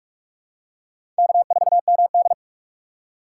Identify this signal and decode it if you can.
Morse code, heard as a keyed tone, about 44 wpm, 700 Hz. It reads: K4MD
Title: USA Call Signs - 4 Characters - 4x Character Spacing - 45wpm